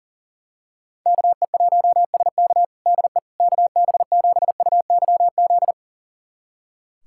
Morse code, heard as a keyed tone, about 40 wpm, 700 Hz. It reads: KE1SK DE KB7UYZ
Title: USA Callsign DE Callsign 40wpm